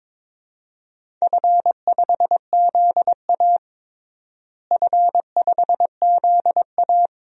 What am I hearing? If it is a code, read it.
Morse code, heard as a keyed tone, about 22 wpm, 700 Hz. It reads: F5ZA F5ZA
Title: DXCC-100 Call Signs - 4 Characters - Repeated 2x - 22wpm